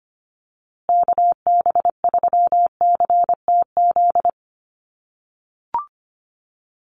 Morse code, heard as a keyed tone, about 25 wpm, 700 Hz. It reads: KB3CTZ
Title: USA Call Signs Repeated 3x 25wpm